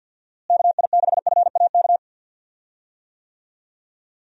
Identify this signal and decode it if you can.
Morse code, heard as a keyed tone, about 45 wpm, 700 Hz. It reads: KI6FAK
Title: USA Call Signs - 6 Characters - Repeated 3x - 50wpm